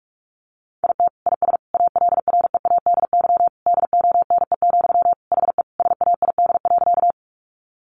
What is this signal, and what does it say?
Morse code, heard as a keyed tone, about 44 wpm, 700 Hz. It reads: IT IS ALREADY DONE, HE SAID.
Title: Sentences from Top 700 Words 45wpm